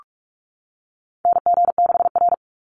Morse code, heard as a keyed tone, about 44 wpm, 700 Hz. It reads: NG6R
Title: NA SOTA Chaser Callsigns 45wpm